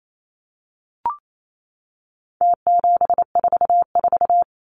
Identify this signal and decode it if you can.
Morse code, heard as a keyed tone, about 28 wpm, 700 Hz. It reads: T744
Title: Numbers - Contest Style - 4-Digit 28wpm